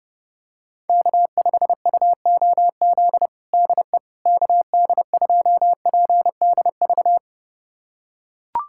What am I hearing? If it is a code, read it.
Morse code, heard as a keyed tone, about 30 wpm, 700 Hz. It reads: K5UOZ DE KD2PDV
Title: USA Callsign DE Callsign 30wpm